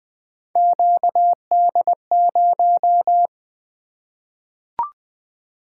Morse code, heard as a keyed tone, about 20 wpm, 700 Hz. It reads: QD0
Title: Three Letter-Number Combinations - 20wpm